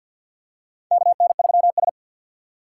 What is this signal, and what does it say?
Morse code, heard as a keyed tone, about 50 wpm, 700 Hz. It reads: KN3S
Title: USA Call Signs - 3x Character Spacing - 50wpm